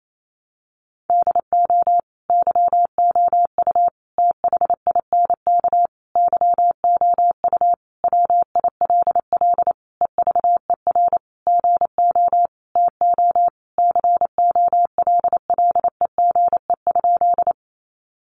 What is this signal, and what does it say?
Morse code, heard as a keyed tone, about 28 wpm, 700 Hz. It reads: DO YOU THINK YOU WILL EVER GO TO COLLEGE?
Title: Sentences from Top 1300 Words 28wpm